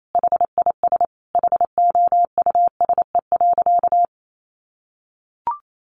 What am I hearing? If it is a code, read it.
Morse code, heard as a keyed tone, about 28 wpm, 700 Hz. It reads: HIS HOUSE.
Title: Sentences from Top 1000 Words 28wpm